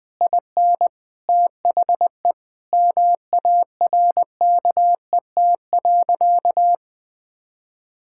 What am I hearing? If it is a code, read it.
Morse code, heard as a keyed tone, about 20 wpm, 700 Hz. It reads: IN THE MARKET.
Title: Sentences from Top 1600 Words 20wpm